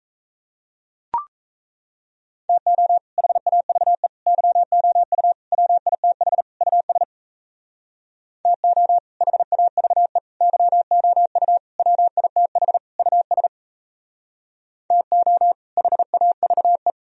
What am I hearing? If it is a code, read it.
Morse code, heard as a keyed tone, about 36 wpm, 700 Hz. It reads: TO HAVE YOU WITH US TO HAVE YOU WITH US TO HAVE